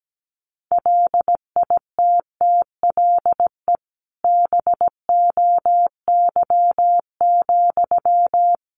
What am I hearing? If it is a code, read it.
Morse code, heard as a keyed tone, about 17 wpm, 700 Hz. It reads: LITTLE BOY,
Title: Sentences from Top 1000 Words 17wpm